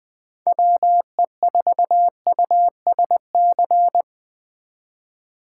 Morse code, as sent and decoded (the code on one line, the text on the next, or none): .-- . ....- ..- ... -.-.
WE4USC